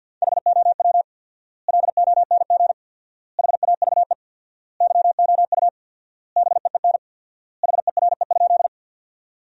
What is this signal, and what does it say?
Morse code, heard as a keyed tone, about 49 wpm, 700 Hz. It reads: HOW LONG HAVE YOU BEEN HERE?
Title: 4x Word-Spacing - Sentences from Top 200 Words - 50wpm